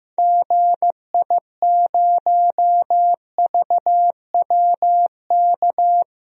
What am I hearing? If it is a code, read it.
Morse code, heard as a keyed tone, about 15 wpm, 700 Hz. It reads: GI0VWK
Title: DXCC-100 Call Signs - 6 Characters - 15wpm